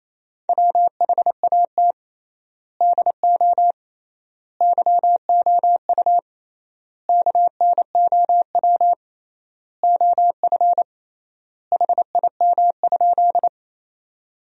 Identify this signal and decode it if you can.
Morse code, heard as a keyed tone, about 28 wpm, 700 Hz. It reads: WHAT DO YOU KNOW OF HIM?
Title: 3x Word-Spacing - Sentences from Top 100 Words - 28wpm